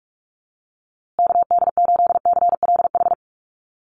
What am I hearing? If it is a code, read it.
Morse code, heard as a keyed tone, about 44 wpm, 700 Hz. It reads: KD8CLH
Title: USA Call Signs Repeated 2x 45wpm